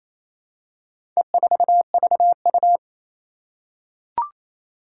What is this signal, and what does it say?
Morse code, heard as a keyed tone, about 28 wpm, 700 Hz. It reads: E4VU
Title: Four Challenging Letter-Number Combinations 28wpm